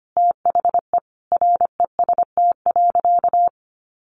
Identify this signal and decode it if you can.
Morse code, heard as a keyed tone, about 25 wpm, 700 Hz. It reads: THE REST.
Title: Sentences from Top 1100 Words 25wpm